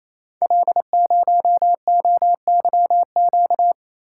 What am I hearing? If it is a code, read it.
Morse code, heard as a keyed tone, about 27 wpm, 700 Hz. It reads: L0OYQ
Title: Code Groups - 5 Letter:Numbers - 28wpm